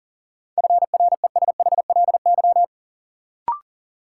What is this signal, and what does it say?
Morse code, heard as a keyed tone, about 40 wpm, 700 Hz. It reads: FRESHLY
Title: Sets of 5 Words Encore - New Word Drill - 40wpm